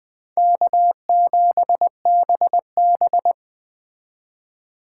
Morse code, heard as a keyed tone, about 20 wpm, 700 Hz. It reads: K7BB
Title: Speed Racing - USA 1x2 Call Signs 17wpm